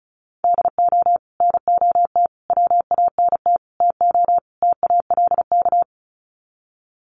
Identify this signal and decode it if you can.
Morse code, heard as a keyed tone, about 34 wpm, 700 Hz. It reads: DO NOT WANT TO TALK
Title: Sets of 5 Words 35wpm